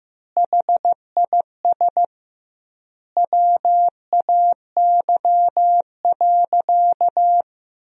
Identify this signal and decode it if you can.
Morse code, heard as a keyed tone, about 15 wpm, 700 Hz. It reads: HIS WAY.
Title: 2x Word-Spacing - Sentences from Top 300 Words - 15wpm